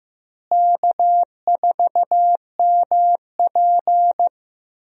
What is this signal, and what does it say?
Morse code, heard as a keyed tone, about 15 wpm, 700 Hz. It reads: K4MP
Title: USA Call Signs - 4 Characters - Repeated 3x - 15wpm